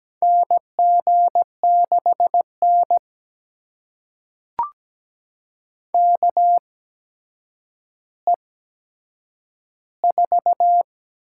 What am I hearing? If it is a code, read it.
Morse code, heard as a keyed tone, about 17 wpm, spaced as for 8 wpm, 700 Hz. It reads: NG6N KE4
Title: USA Call Signs - 8x Character Spacing - 17wpm